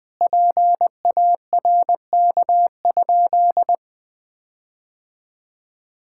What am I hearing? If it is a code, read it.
Morse code, heard as a keyed tone, about 21 wpm, 700 Hz. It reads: PARK?
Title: Sentences with 4-Word Phrases - Spoken Beginning - 20wpm